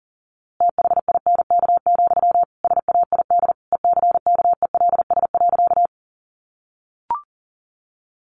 Morse code, heard as a keyed tone, about 40 wpm, 700 Hz. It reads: THINK, SAID ECKELS.
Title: A Sound Of Thunder 40wpm